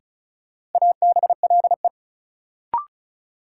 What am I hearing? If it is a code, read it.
Morse code, heard as a keyed tone, about 36 wpm, 700 Hz. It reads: ABLE
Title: Sets of 2 Words - New Word Drill - 35wpm